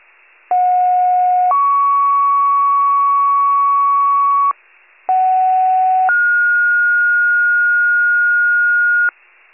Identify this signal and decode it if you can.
Two-tone page, heard as a one-second tone, then a three-second tone, about 730 then 1120 Hz, about 750 then 1430 Hz, with noise. It is heard from 0.5 to 9.1 s.
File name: pager / 57 FIRE / wcalert